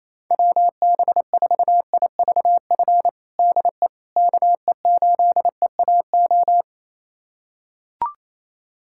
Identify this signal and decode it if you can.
Morse code, heard as a keyed tone, about 28 wpm, 700 Hz. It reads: WB4IVF DE KE8EAO
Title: USA Callsign DE Callsign 28wpm